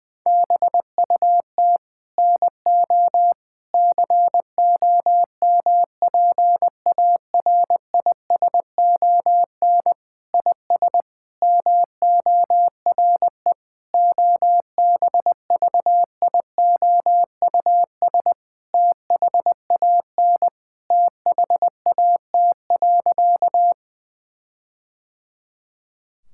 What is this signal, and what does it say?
Morse code, heard as a keyed tone, about 21 wpm, 700 Hz. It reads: BUT NO COMPARISON IS MORE OBVIOUS THAN THAT.
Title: Sentences from Top 1600 Words 20wpm